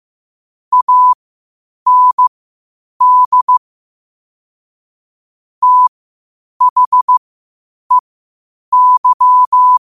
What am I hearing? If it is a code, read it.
Morse code, heard as a keyed tone, about 15 wpm, spaced as for 8 wpm, 1000 Hz. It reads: AND THEY
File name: qotdfiles / 8wpm